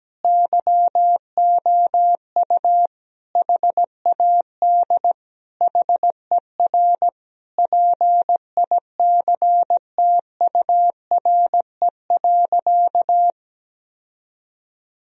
Morse code, heard as a keyed tone, about 17 wpm, 700 Hz. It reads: YOU HAD HER PICTURE.
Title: Sentences from Top 1000 Words - Review Part 1 - 17wpm